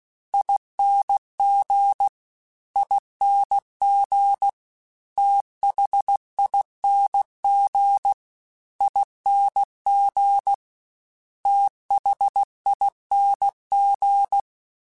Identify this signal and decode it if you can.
Morse code, heard as a keyed tone, about 16 wpm, 800 Hz. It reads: ING ING THING ING THING
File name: Special INGs-mp3s / Special INGs-mp3s / Slow ing @15